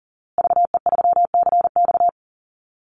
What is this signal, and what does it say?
Morse code, heard as a keyed tone, about 40 wpm, 700 Hz. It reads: VE3CX
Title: Call Signs Repeated 2x - 40wpm